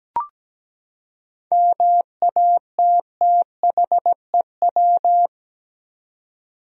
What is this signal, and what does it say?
Morse code, heard as a keyed tone, about 17 wpm, 700 Hz. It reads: MATTHEW
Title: USA Common Names - 17wpm